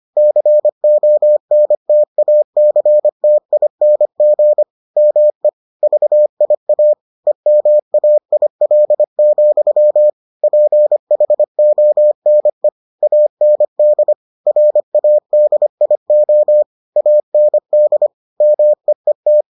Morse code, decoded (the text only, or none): CONTACTING ME VIA EMAIL, PHONE AND RADIO AND MEET